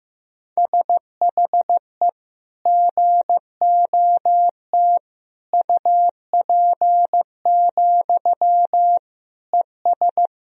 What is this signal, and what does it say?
Morse code, heard as a keyed tone, about 15 wpm, 700 Hz. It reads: SHE GOT UP, ES
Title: Sentences from Top 500 Words and Abbreviations 15wpm